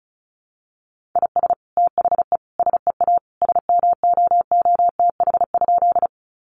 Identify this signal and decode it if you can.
Morse code, heard as a keyed tone, about 34 wpm, 700 Hz. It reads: IS THE SEA SMOOTH?
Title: Sentences from Top 1400 Words 35wpm